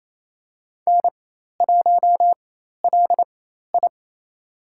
Morse code, heard as a keyed tone, about 28 wpm, spaced as for 13 wpm, 700 Hz. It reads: N1LI
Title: USA Call Signs - 4 Characters - 4x Character Spacing - 28wpm